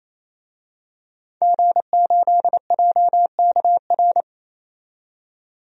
Morse code, heard as a keyed tone, about 27 wpm, 700 Hz. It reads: G8JKR
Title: DXCC-100 Call Signs - 5 Characters - 28wpm